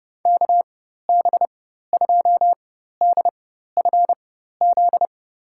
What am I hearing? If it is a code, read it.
Morse code, heard as a keyed tone, about 30 wpm, spaced as for 14 wpm, 700 Hz. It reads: KB2DFZ
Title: USA Call Signs - 6 Characters - 4x Character Spacing - 30wpm